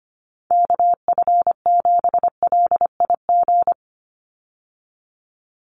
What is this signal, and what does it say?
Morse code, heard as a keyed tone, about 25 wpm, 700 Hz. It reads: KF7LIG
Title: USA Call Signs 25wpm